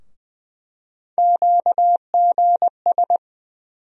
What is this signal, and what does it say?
Morse code, heard as a keyed tone, about 20 wpm, 700 Hz. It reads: QGS
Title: USA Call Sign Postfixes - 3 Letters 20wpm